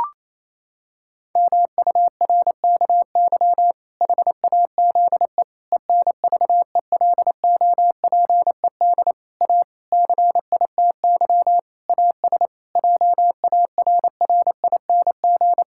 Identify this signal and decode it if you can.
Morse code, heard as a keyed tone, about 28 wpm, 700 Hz. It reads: MURKY HAZE ENVELOPED A CITY AS JARRING
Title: Pangram Challenge v2 - 28wpm